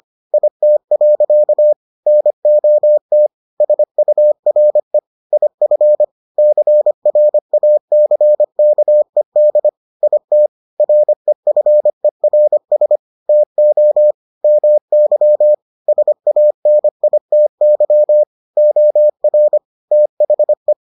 Morse code, heard as a keyed tone, about 25 wpm, 600 Hz. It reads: IT. NOT SURE IF CRACKED IT REFERS TO MY SANITY OR THE